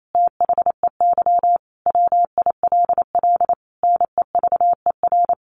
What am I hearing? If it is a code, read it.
Morse code, heard as a keyed tone, about 28 wpm, 700 Hz. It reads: THEY WILL NEVER
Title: Sentences from Top 1400 Words 28wpm